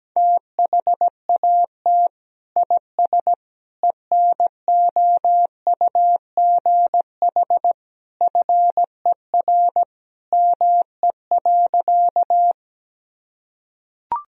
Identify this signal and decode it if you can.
Morse code, heard as a keyed tone, about 17 wpm, 700 Hz. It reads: THAT IS ENOUGH FER ME.
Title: Sentences from Top 400 Words and Abbreviations 17wpm